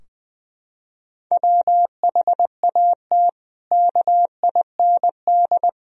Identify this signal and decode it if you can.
Morse code, heard as a keyed tone, about 20 wpm, 700 Hz. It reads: WHAT KIND
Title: Sentences from Top 1000 Words - Review Part 4 - 20wpm